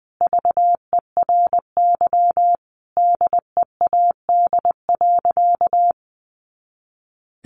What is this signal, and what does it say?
Morse code, heard as a keyed tone, about 20 wpm, 700 Hz. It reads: VERY DEAD.
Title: A Sound Of Thunder 20wpm